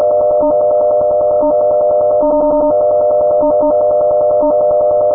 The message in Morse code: none